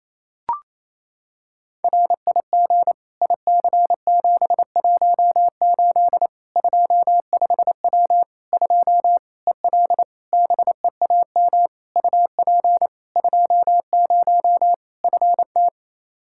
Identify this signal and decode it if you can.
Morse code, heard as a keyed tone, about 28 wpm, 700 Hz. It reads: RIG IC718 25W 2 EL BEAM UP 20 FT
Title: Practice QSO - Exchange 2 - Info - 28wpm